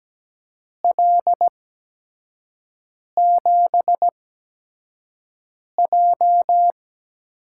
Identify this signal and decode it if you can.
Morse code, heard as a keyed tone, about 17 wpm, 700 Hz. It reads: L 7 J
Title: USA Call Signs - 6 Characters - 8x Character Spacing - 17wpm